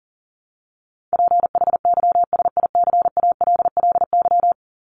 Morse code, heard as a keyed tone, about 40 wpm, 700 Hz. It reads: PHYSICALLY
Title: Top 1501-1750 Words 40wpm